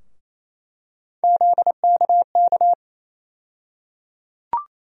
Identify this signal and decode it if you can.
Morse code, heard as a keyed tone, about 28 wpm, 700 Hz. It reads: ZKK
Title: USA Call Sign Postfixes - 3 Letters 28wpm